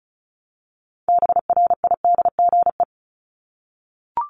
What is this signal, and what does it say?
Morse code, heard as a keyed tone, about 34 wpm, 700 Hz. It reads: BRIDGE
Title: Top 1001-1250 Words 35wpm